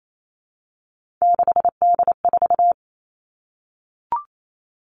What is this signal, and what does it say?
Morse code, heard as a keyed tone, about 28 wpm, 700 Hz. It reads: 6D4
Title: Three Challenging Letter-Number Combinations 28wpm